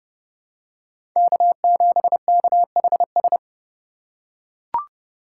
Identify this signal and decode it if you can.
Morse code, heard as a keyed tone, about 31 wpm, 700 Hz. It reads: K7KHS
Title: USA Call Signs - 5 Characters - 4x Character Spacing - 30wpm